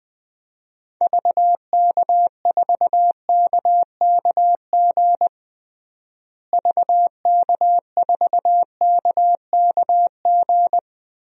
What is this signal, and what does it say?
Morse code, heard as a keyed tone, about 21 wpm, 700 Hz. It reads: VK4KKG VK4KKG
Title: DXCC-100 Call Signs - 6 Characters - Repeated 2x - 20wpm